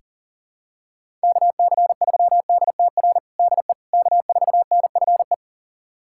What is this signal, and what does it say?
Morse code, heard as a keyed tone, about 37 wpm, 700 Hz. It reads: KC3DTR DE K4NFE
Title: USA Callsign DE Callsign 40wpm